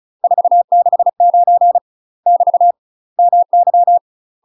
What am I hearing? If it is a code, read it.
Morse code, heard as a keyed tone, about 36 wpm, 700 Hz. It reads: 469 <BT> MY
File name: UK MASTER QSOs TRAINING / 35WPM / UK MASTER QSOs 0031